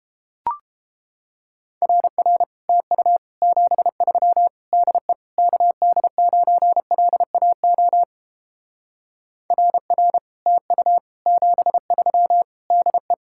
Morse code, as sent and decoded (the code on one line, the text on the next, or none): .-. .-. / - ..- / --... ...-- / -.. / -.- -.. ----. .-.. .- --- / .-. .-. / - ..- / --... ...-- / -.. .
RR TU 73 DE KD9LAO RR TU 73 DE